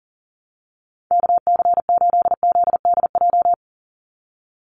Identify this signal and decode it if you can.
Morse code, heard as a keyed tone, about 40 wpm, 690 Hz. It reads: KC8ZDJ
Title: USA Call Signs 40wpm